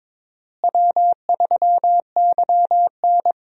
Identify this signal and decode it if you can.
Morse code, heard as a keyed tone, about 22 wpm, 700 Hz. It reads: W3YN
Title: USA Call Signs - 4 Characters - Repeated 3x - 22wpm